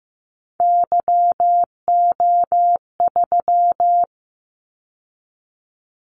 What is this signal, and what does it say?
Morse code, heard as a keyed tone, about 15 wpm, 700 Hz. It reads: YO3
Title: Callsign Prefixes for Top 25 DXCC - 15wpm